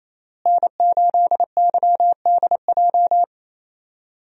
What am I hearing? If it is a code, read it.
Morse code, heard as a keyed tone, about 28 wpm, 700 Hz. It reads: N8YDJ
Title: USA Call Signs - 5 Characters - Repeated 3x - 28wpm